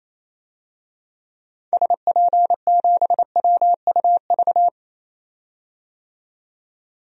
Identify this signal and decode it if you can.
Morse code, heard as a keyed tone, about 28 wpm, 700 Hz. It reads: SP7WUV
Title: DXCC-100 Call Signs - 6 Characters - 28wpm